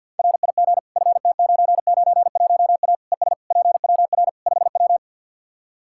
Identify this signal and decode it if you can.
Morse code, heard as a keyed tone, about 51 wpm, 700 Hz. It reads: RIG FT991A ES PWR 5W
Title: Practice QSO - Exchange 2 - Info - 50wpm